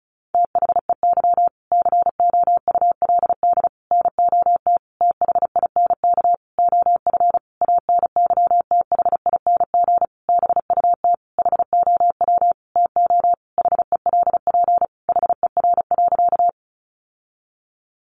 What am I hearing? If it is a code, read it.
Morse code, heard as a keyed tone, about 34 wpm, 700 Hz. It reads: THEY COULD NOT THINK OF ANYTHING BUT HOW TO HELP HER.
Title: Sentences from Top 1000 Words 35wpm